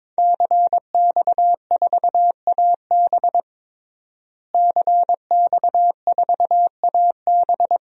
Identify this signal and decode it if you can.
Morse code, heard as a keyed tone, about 22 wpm, 700 Hz. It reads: CX4AB CX4AB
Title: DXCC-100 Call Signs - 5 Characters - Repeated 3x - 22wpm